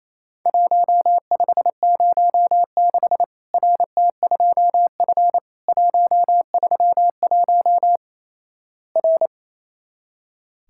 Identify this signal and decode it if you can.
Morse code, heard as a keyed tone, about 28 wpm, 700 Hz. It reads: 1506 RT2F 131 R
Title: Worked All Europe DX Contest - QTC Exchange - 28wpm